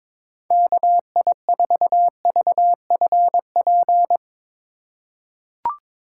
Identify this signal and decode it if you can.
Morse code, heard as a keyed tone, about 21 wpm, 700 Hz. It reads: KI4VFP
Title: USA Call Signs - 2x Character Spacing - 22wpm